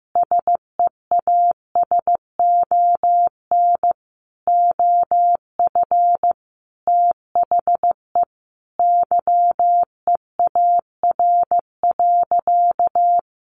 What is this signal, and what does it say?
Morse code, heard as a keyed tone, about 15 wpm, 700 Hz. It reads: SEASON OF THE YEAR.
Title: Sentences from Top 1400 Words 15wpm